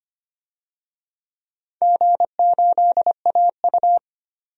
Morse code, heard as a keyed tone, about 25 wpm, 700 Hz. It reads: G8AU
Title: DXCC-100 Call Signs - 4 Characters - 25wpm